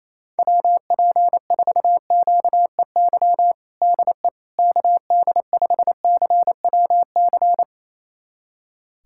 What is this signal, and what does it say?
Morse code, heard as a keyed tone, about 28 wpm, 700 Hz. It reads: WP4QEY DE KD5CWC
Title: USA Callsign DE Callsign 28wpm